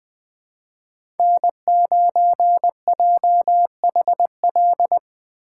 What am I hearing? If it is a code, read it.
Morse code, heard as a keyed tone, about 21 wpm, 700 Hz. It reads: N9JHL
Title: K1USN SST QSO Practice - 20wpm